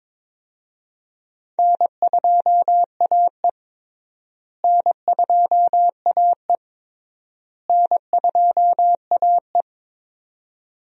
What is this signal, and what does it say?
Morse code, heard as a keyed tone, about 22 wpm, 700 Hz. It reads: N2AE N2AE N2AE
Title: USA Call Signs - 4 Characters - Repeated 3x - 22wpm